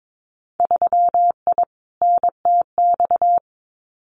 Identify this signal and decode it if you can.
Morse code, heard as a keyed tone, about 22 wpm, 700 Hz. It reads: 3I NTX
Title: Winter Field Day Exchanges 22wpm